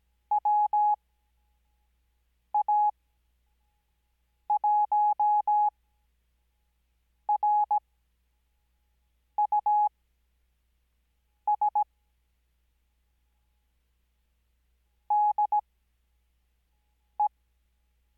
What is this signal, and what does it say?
Morse code, heard as a keyed tone, about 17 wpm, spaced as for 5 wpm, 840 Hz. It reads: WA1RUS DE
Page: Farnsworth Spacing: 15WPM Character speed Spaced at 5WPM.